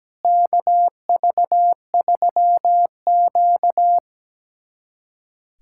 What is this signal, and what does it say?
Morse code, heard as a keyed tone, about 17 wpm, 700 Hz. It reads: KV3Q
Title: USA Callsign DE Callsign - 4 Characters - 17wpm